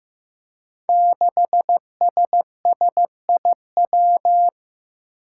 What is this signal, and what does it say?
Morse code, heard as a keyed tone, about 15 wpm, 700 Hz. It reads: 6SSIW
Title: Code Groups - 5 Letter:Numbers - 15wpm